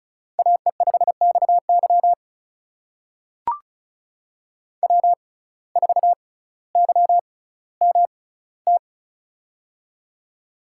Morse code, heard as a keyed tone, about 36 wpm, spaced as for 18 wpm, 700 Hz. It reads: AE5XY W4YMT
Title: USA Call Signs - 5 Characters - 6x Character Spacing - 35wpm